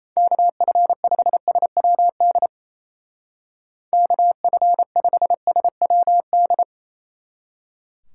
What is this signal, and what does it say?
Morse code, heard as a keyed tone, about 36 wpm, 700 Hz. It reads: KF5SWD KF5SWD